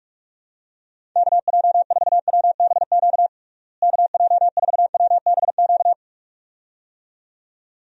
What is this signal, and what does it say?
Morse code, heard as a keyed tone, about 46 wpm, 700 Hz. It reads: KJ4WBQ KJ4WBQ
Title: USA Call Signs - 6 Characters - Repeated 2x - 45wpm